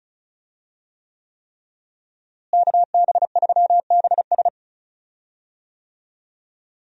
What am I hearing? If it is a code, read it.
Morse code, heard as a keyed tone, about 34 wpm, 700 Hz. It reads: KB3BS
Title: USA Call Signs - 5 Characters - Repeated 3x - 35wpm